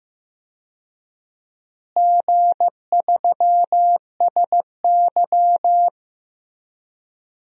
Morse code, heard as a keyed tone, about 16 wpm, 700 Hz. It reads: G3SY
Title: DXCC-100 Call Signs - 4 Characters - Repeated 2x - 15wpm